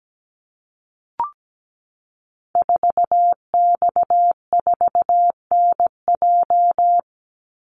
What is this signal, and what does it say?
Morse code, heard as a keyed tone, about 17 wpm, 700 Hz. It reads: 4X4NJ
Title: Call Signs - 17wpm